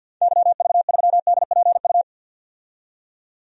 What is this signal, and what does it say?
Morse code, heard as a keyed tone, about 50 wpm, 700 Hz. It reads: YV3DPU
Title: DXCC-100 Call Signs - 6 Characters - 50wpm